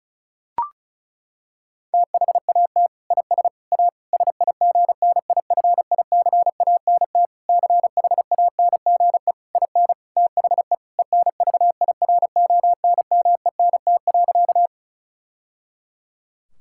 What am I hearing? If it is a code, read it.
Morse code, heard as a keyed tone, about 36 wpm, 700 Hz. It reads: THAT IS A SIGNIFICANT CHANGE IN THE ENVIRONMENT.
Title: Sentences from Top 1000 Words - Review Part 6 - 35wpm